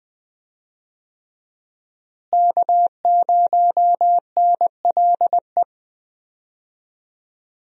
Morse code, heard as a keyed tone, about 21 wpm, 700 Hz. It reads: K0NLE
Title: USA Call Signs - 5 Characters - Repeated 3x - 20wpm